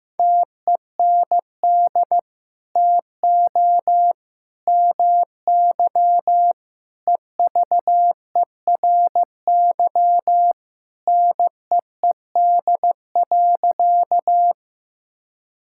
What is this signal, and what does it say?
Morse code, heard as a keyed tone, about 15 wpm, 700 Hz. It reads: TEND TO MY EVERY NEED.
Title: Sentences from Top 1000 Words - Review Part 5 - 15wpm